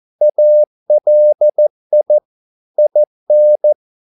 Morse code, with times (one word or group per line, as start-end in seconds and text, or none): 0.2-2.2 ALI
2.8-3.7 IN